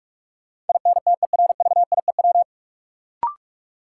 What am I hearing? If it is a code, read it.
Morse code, heard as a keyed tone, about 46 wpm, 700 Hz. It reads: INTERVIEW
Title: Sentences from Top 1600 Words - New Word Drill - 45wpm